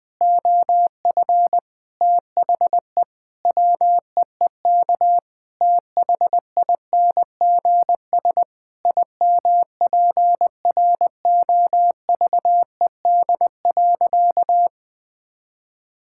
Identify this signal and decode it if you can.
Morse code, heard as a keyed tone, about 20 wpm, 700 Hz. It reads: OF THE WEEK THINGS IMPROVED.
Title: Sentences with 4-Word Phrases - Spoken Beginning - 20wpm